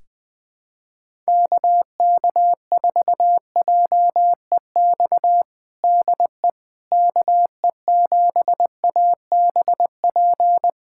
Morse code, heard as a keyed tone, about 20 wpm, 700 Hz. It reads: KK4JEX DE KE7ABP
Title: USA Callsign DE Callsign - 6 Characters - 20wpm